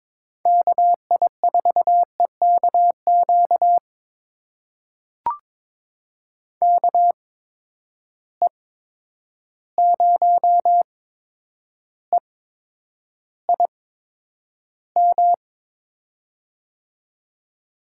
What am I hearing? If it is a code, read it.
Morse code, heard as a keyed tone, about 22 wpm, spaced as for 9 wpm, 700 Hz. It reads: KI4EKQ KE0EIM
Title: USA Call Signs - 6 Characters - 8x Character Spacing - 22wpm